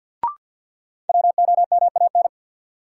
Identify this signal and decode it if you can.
Morse code, heard as a keyed tone, about 51 wpm, 700 Hz. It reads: WOMAN
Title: Sentences from Top 1100 Words - New Word Drill Alt - 50wpm